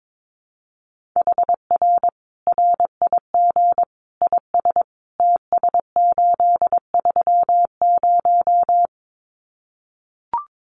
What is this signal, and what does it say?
Morse code, heard as a keyed tone, about 22 wpm, 700 Hz. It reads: HR RIG IS TS830
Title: Common QSO Phrases 22wpm